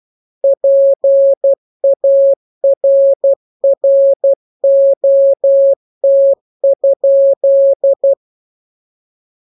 Morse code, heard as a keyed tone, about 12 wpm, 550 Hz. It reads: PARROT?
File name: jokes / joke 12 12 0000